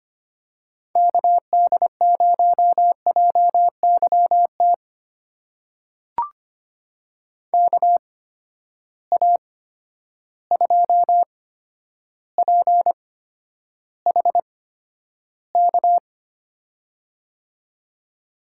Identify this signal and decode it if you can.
Morse code, heard as a keyed tone, about 25 wpm, spaced as for 11 wpm, 700 Hz. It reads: KD0JYT KA2PHK
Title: USA Call Signs - 8x Character Spacing - 25wpm